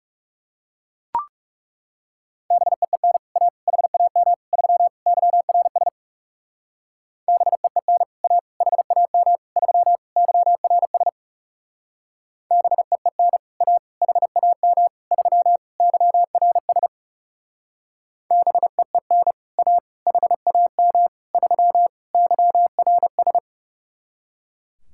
Morse code, heard as a keyed tone, about 38 wpm, 700 Hz. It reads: BEEN A HAM 3 YRS BEEN A HAM 3 YRS BEEN A HAM 3 YRS BEEN A HAM 3 YRS